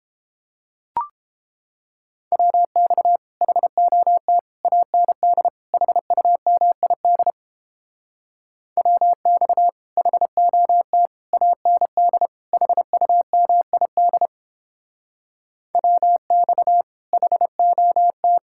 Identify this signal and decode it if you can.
Morse code, heard as a keyed tone, about 29 wpm, 700 Hz. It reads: WX HOT AND HUMID WX HOT AND HUMID WX HOT